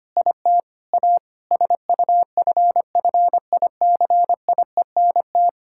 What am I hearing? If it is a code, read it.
Morse code, heard as a keyed tone, about 25 wpm, 700 Hz. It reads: IT A SUFFICIENT
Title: Sentences from Top 1600 Words 25wpm